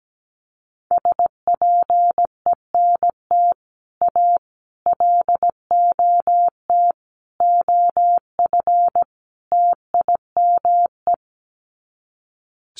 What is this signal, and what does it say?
Morse code, heard as a keyed tone, about 17 wpm, 700 Hz. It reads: SPENT A LOT OF TIME
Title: Sets of 5 Words 17wpm